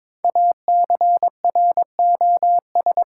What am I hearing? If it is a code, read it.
Morse code, heard as a keyed tone, about 22 wpm, 700 Hz. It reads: ACROS
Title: Sentences from Top 700 Words - New Word Drill - 22wpm